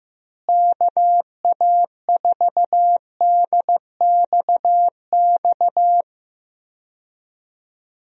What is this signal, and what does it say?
Morse code, heard as a keyed tone, about 15 wpm, 700 Hz. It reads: KA4DXX
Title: USA Call Signs - 6 Characters - Repeated 3x - 15wpm